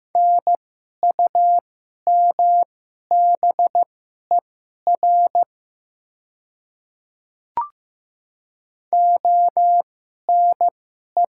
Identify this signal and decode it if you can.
Morse code, heard as a keyed tone, about 15 wpm, spaced as for 11 wpm, 700 Hz. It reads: NUMBER ONE
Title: 2x Character Spacing - Top 100 Words - 15wpm